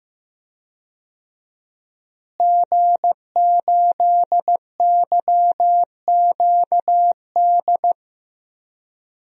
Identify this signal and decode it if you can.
Morse code, heard as a keyed tone, about 15 wpm, 700 Hz. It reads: G8YQD
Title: DXCC-100 Call Signs - 5 Characters - Repeated 2x - 15wpm